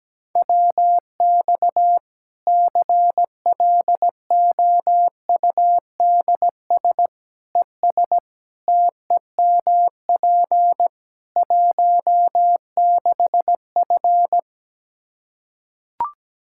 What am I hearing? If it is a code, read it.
Morse code, heard as a keyed tone, about 17 wpm, 700 Hz. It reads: WX CLOUDS ES TEMP 16F
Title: Practice QSO - Exchange 2 - Info - 17wpm